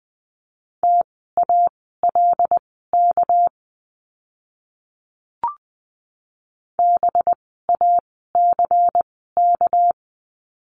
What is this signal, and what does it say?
Morse code, heard as a keyed tone, about 20 wpm, spaced as for 15 wpm, 700 Hz. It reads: TALK BACK
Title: 244 Lesson 39f - Letter z - Words 20wpm